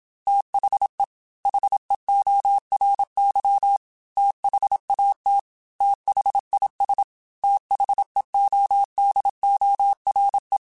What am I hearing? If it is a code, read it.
Morse code, heard as a keyed tone, about 26 wpm, 800 Hz. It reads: THE HEORY THAT THIS THEODORE
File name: TH Runs / TH Runs mp3s / TH Runs @25